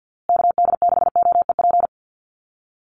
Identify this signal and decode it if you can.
Morse code, heard as a keyed tone, about 50 wpm, 700 Hz. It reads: KD6OEP
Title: USA Call Signs Repeated 2x 50wpm